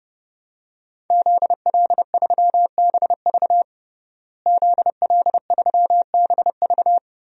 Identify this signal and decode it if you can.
Morse code, heard as a keyed tone, about 30 wpm, 700 Hz. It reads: ZL3BV ZL3BV
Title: DXCC-100 Call Signs - 5 Characters - Repeated 3x - 30wpm